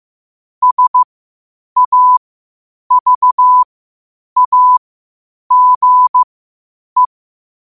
Morse code, heard as a keyed tone, about 15 wpm, spaced as for 8 wpm, 1000 Hz. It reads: SAVAGE
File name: qotdfiles / 8wpm